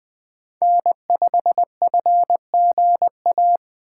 Morse code, heard as a keyed tone, about 20 wpm, 700 Hz. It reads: N5FGA
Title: USA Call Signs - 5 Characters - 4x Character Spacing - 20wpm